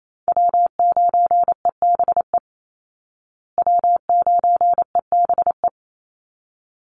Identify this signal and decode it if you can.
Morse code, heard as a keyed tone, about 28 wpm, 700 Hz. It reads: W9EBE W9EBE
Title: Call Signs Repeated 3x - 28wpm